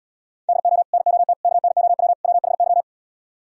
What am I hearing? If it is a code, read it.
Morse code, heard as a keyed tone, about 77 wpm, 700 Hz. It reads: DO NOT KNOW WHY